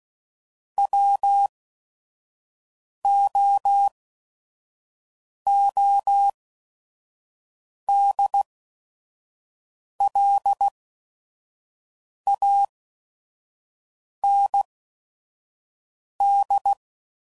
Morse code, heard as a keyed tone, about 16 wpm, spaced as for 5 wpm, 800 Hz. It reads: WOODLAND
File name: CW Code Groups @5WPM / fromSM's / Cities